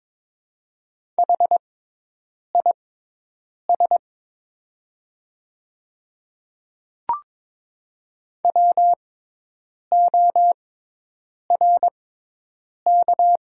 Morse code, heard as a keyed tone, about 23 wpm, spaced as for 8 wpm, 700 Hz. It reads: HIS WORK
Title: 6x Character Spacing - Top 200 Words - 22wpm